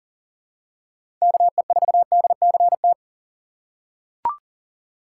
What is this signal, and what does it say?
Morse code, heard as a keyed tone, about 40 wpm, 700 Hz. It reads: KE4DCT
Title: USA Call Signs - 2x Character Spacing - 40wpm